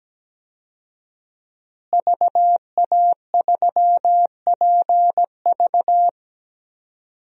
Morse code, heard as a keyed tone, about 17 wpm, 700 Hz. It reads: VA3PV
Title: DXCC-100 Call Signs - 5 Characters - Repeated 3x - 17wpm